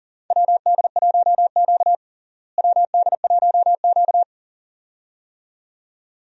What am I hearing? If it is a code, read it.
Morse code, heard as a keyed tone, about 38 wpm, 700 Hz. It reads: WD1Q WD1Q
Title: USA Call Signs - 4 Characters - Repeated 2x - 40wpm